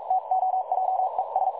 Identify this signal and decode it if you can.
Morse code, heard as a keyed tone, about 42 wpm, spaced as for 36 wpm, 750 Hz, with noise. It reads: TOJES
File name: dlazaawansowanych / nagraniazpasma / websdr recording 3565.0kHz